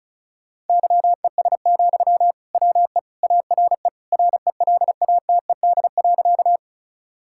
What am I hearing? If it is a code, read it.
Morse code, heard as a keyed tone, about 36 wpm, 700 Hz. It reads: YES, WE ARE RELATED.
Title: Sentences from Top 1000 Words - Review Part 5 - 35wpm